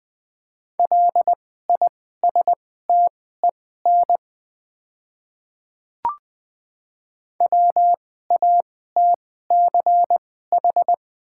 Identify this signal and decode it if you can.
Morse code, heard as a keyed tone, about 20 wpm, spaced as for 14 wpm, 690 Hz. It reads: LISTEN WATCH
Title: 2x Character Spacing - Top 500 Words - 20wpm